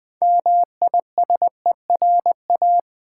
Morse code, heard as a keyed tone, about 20 wpm, 700 Hz. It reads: MISERA
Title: Build A Word - 3 Letters At A Time - 9-Letter Words - 20wpm